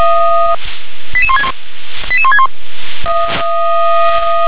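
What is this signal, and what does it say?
Five-tone selective call (ZVEI), heard as quick runs of five tones, tones of about 70 ms. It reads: 88161 88161
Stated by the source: ZVEI five-tone call 88164 received by SDR.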